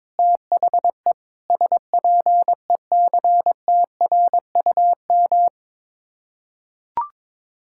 Morse code, heard as a keyed tone, about 21 wpm, 700 Hz. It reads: THE SPECTRUM
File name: practice / 22wpm / Sets of 7 Words - 22wpm